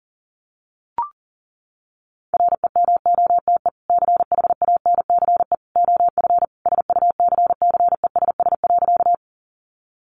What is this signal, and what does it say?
Morse code, heard as a keyed tone, about 37 wpm, 700 Hz. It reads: REMOTE CHANCE OF SUCCESS.
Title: Sentences from Top 1300 Words 40wpm